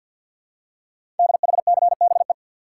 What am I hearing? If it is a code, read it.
Morse code, heard as a keyed tone, about 48 wpm, 700 Hz. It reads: DHCBE
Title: Code Groups - 5 Letters - 50wpm